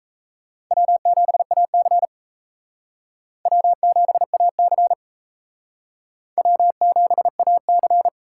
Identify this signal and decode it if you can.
Morse code, heard as a keyed tone, about 35 wpm, 700 Hz. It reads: W7AC W7AC W7AC
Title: Speed Racing - USA 1x2 Call Signs 28wpm